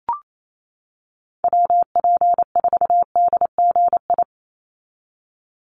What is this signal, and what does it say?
Morse code, heard as a keyed tone, about 28 wpm, 700 Hz. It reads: WP4DGI
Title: USA Call Signs 28wpm